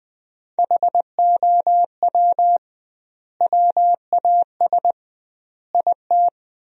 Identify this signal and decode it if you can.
Morse code, heard as a keyed tone, about 20 wpm, 700 Hz. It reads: HOW WAS IT
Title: 2x Word-Spacing - Sentences from Top 500 Words - 20wpm